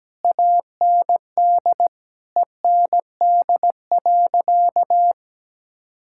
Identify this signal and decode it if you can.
Morse code, heard as a keyed tone, about 17 wpm, 700 Hz. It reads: AND END.
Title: Sentences from Top 1000 Words - Review Part 6 - 17wpm